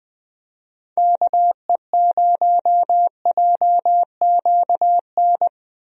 Morse code, heard as a keyed tone, about 21 wpm, 700 Hz. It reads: KE0JQN
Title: USA Call Signs - 6 Characters - 20wpm